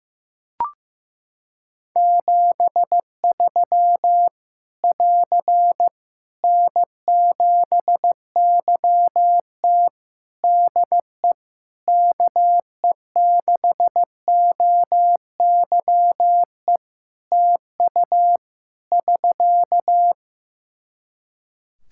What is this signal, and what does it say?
Morse code, heard as a keyed tone, about 15 wpm, 700 Hz. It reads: 73 <AR> N7YT DE KE6OYE TU <SK>
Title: Practice QSO - Exchange 4 - Ending - 15wpm